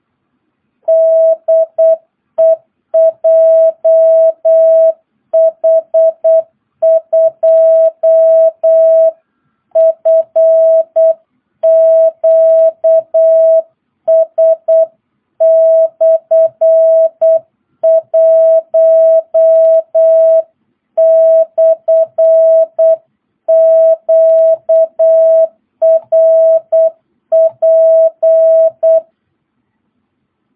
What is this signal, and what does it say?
Morse code, heard as a keyed tone, about 8 wpm, 650 Hz. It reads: DE JH2FQS/1/QRP
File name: jh8cbh / 2022cw